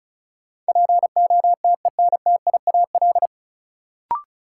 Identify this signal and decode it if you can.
Morse code, heard as a keyed tone, about 34 wpm, 700 Hz. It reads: POTENTIAL
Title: Sentences from Top 1000 Words - Review Part 4 - New Word Drill - 35wpm